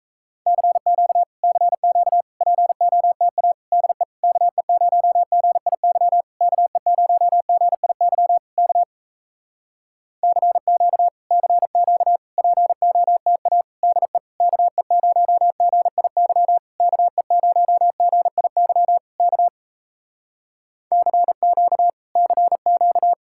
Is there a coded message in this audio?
Yes: Morse code from 0.5 to 23.1 s, heard as a keyed tone, about 38 wpm, 700 Hz. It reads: CQ CQ POTA DE KE0GIY KE0GIY K CQ CQ POTA DE KE0GIY KE0GIY K CQ CQ